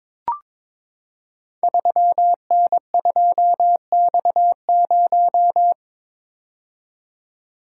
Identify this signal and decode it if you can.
Morse code, heard as a keyed tone, about 22 wpm, 700 Hz. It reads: 3N2X0
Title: Code Groups - 5 Letter:Numbers - 22wpm